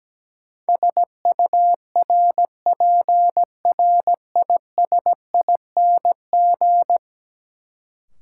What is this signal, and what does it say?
Morse code, heard as a keyed tone, about 17 wpm, 700 Hz. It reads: SURPRISING
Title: Sets of 5 Words - New Word Drill - 17wpm